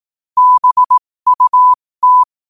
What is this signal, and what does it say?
Morse code, heard as a keyed tone, about 18 wpm, spaced as for 15 wpm, 1000 Hz. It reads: BUT
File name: qotdfiles / 15wpm